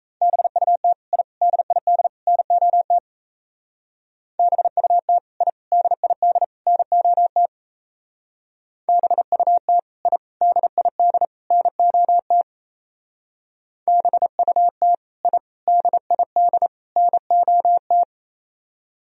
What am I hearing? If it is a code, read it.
Morse code, heard as a keyed tone, about 35 wpm, 700 Hz. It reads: BUT I DID NOT BUT I DID NOT BUT I DID NOT BUT I DID NOT